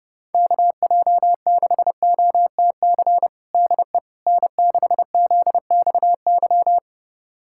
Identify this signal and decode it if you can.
Morse code, heard as a keyed tone, about 30 wpm, 690 Hz. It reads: KJ6OTC DE N6ZXY
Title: USA Callsign DE Callsign 30wpm